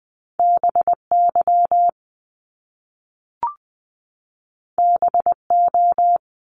Morse code, heard as a keyed tone, about 20 wpm, 700 Hz. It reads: BY BO
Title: Top Two Letter Combinations in Words 20wpm